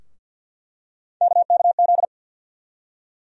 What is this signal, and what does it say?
Morse code, heard as a keyed tone, about 48 wpm, 700 Hz. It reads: KKZ
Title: USA Call Sign Postfixes - 3 Letters 50wpm